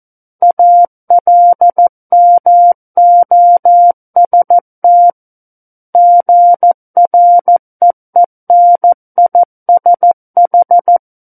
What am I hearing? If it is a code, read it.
Morse code, heard as a keyed tone, about 14 wpm, 700 Hz. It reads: ALMOST GREENISH
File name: Book One / 05-Chapter 5 - 14 WPM